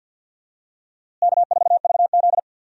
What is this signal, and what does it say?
Morse code, heard as a keyed tone, about 46 wpm, 700 Hz. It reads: K4VZ
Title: USA Call Signs - 4 Characters - Repeated 3x - 50wpm